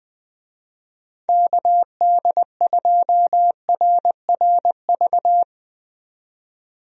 Morse code, heard as a keyed tone, about 21 wpm, 700 Hz. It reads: KD2RRV
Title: USA Call Signs - 6 Characters - 20wpm